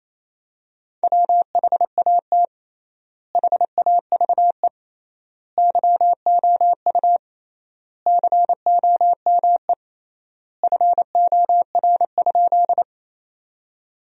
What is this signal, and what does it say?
Morse code, heard as a keyed tone, about 28 wpm, 700 Hz. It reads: WHAT HAVE YOU COME FOR?
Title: 3x Word-Spacing - Sentences from Top 100 Words - 28wpm